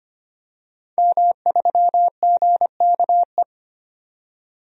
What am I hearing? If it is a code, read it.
Morse code, heard as a keyed tone, about 25 wpm, 700 Hz. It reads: M3GKE
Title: DXCC-100 Call Signs - 5 Characters - 25wpm